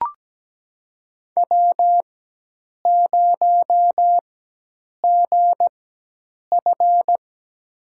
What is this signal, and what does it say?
Morse code, heard as a keyed tone, about 17 wpm, spaced as for 8 wpm, 700 Hz. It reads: W0GF
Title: USA Call Signs - 4 Characters - 4x Character Spacing - 17wpm